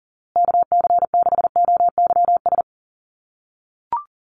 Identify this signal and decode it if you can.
Morse code, heard as a keyed tone, about 40 wpm, 700 Hz. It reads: KC6OYS
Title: USA Call Signs Repeated 2x 40wpm